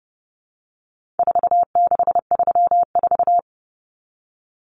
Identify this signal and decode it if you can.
Morse code, heard as a keyed tone, about 30 wpm, 700 Hz. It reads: 4634
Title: Numbers - 4-Digit 30wpm